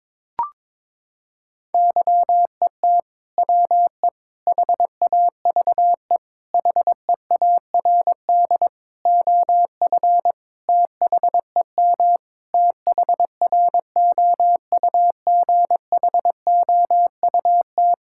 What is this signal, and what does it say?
Morse code, heard as a keyed tone, about 22 wpm, 700 Hz. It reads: YET WE HAVE HEARD OF THEM THROUGHOUT
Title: Sentences from Top 1000 Words - Review Part 5 - 22wpm